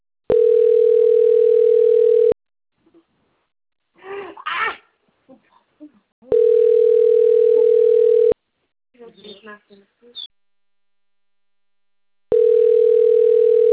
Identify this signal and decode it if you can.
Ringback tone (North America), 440 + 480 Hz, with speech.
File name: rg-1600-unknown- - - .13487